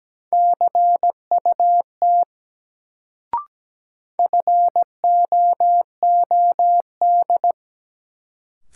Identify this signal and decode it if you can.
Morse code, heard as a keyed tone, about 17 wpm, 700 Hz. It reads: CUT FOOD
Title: Sentences from Top 300 Words - New Word Drill - 17wpm